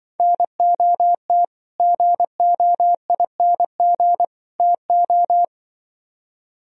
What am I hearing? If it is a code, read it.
Morse code, heard as a keyed tone, about 24 wpm, 700 Hz. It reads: NOT GOING TO
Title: Speed-Racing - Sets of 3 Words 20wpm